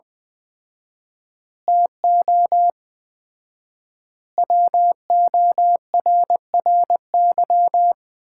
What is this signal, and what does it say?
Morse code, heard as a keyed tone, about 21 wpm, 700 Hz. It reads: TO WORRY
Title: 4x Word-Spacing - Sets of 5 Words - 20wpm